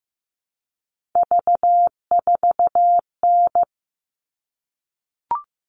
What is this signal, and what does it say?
Morse code, heard as a keyed tone, about 15 wpm, 700 Hz. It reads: V4N
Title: All Cut Numbers - 3-Digit 15wpm